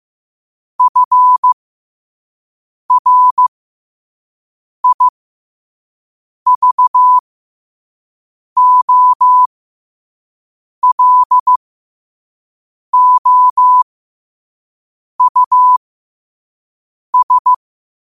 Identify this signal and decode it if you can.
Morse code, heard as a keyed tone, about 15 wpm, spaced as for 5 wpm, 1000 Hz. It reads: FRIVOLOUS
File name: qotdfiles / 5wpm